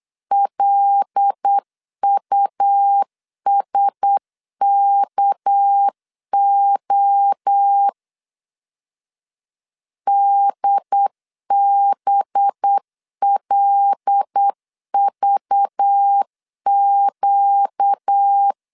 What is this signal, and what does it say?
Morse code, heard as a keyed tone, about 8 wpm, 800 Hz. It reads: LUSKO DBLVQ
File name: LEZIONI / 6 PKX0 / 6 PK 8wpm 3003